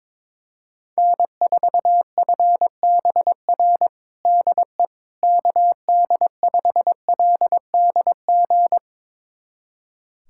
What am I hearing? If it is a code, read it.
Morse code, heard as a keyed tone, about 22 wpm, 700 Hz. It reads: N4FBR DE KD5LDG
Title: USA Callsign DE Callsign 22wpm